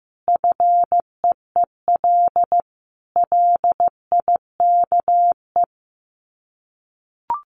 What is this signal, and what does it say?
Morse code, heard as a keyed tone, about 15 wpm, 700 Hz. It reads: FEEL LIKE
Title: Sets of 5 Words Encore 15wpm